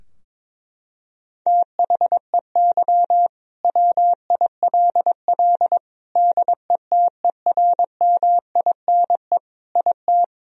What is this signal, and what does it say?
Morse code, heard as a keyed tone, about 21 wpm, 700 Hz. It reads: THEY WILL DETERMINE IT
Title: Sentences from Top 1000 Words - Review Part 4 - 22wpm